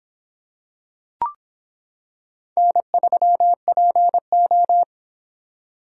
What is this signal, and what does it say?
Morse code, heard as a keyed tone, about 26 wpm, 700 Hz. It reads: N3PO